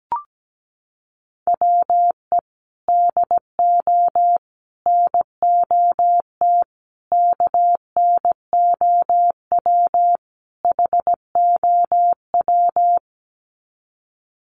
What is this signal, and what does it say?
Morse code, heard as a keyed tone, about 17 wpm, 700 Hz. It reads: WE DO NOT KNOW HOW
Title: Sets of 5 Words Encore 17wpm